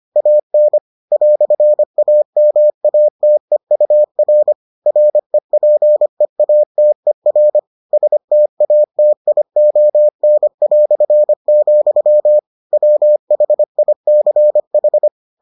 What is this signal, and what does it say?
Morse code, heard as a keyed tone, about 25 wpm, 600 Hz. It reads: AN "AMATEUR REPEATER STATION", WHICH